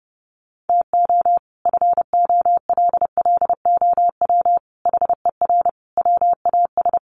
Morse code, heard as a keyed tone, about 30 wpm, 700 Hz. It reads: TO FOLLOW HER WAS
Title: Sentences from Top 1100 Words 30wpm